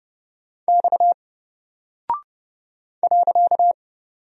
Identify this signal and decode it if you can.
Morse code, heard as a keyed tone, about 30 wpm, 700 Hz. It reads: X .
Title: Single Character (Letter:Number:Punctuation:Prosign) Alt 30wpm